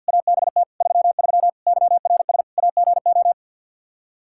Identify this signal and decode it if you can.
Morse code, heard as a keyed tone, about 47 wpm, 700 Hz. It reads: ABT 33 YRS AGO
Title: Practice QSO - Exchange 3 - More Info - 50wpm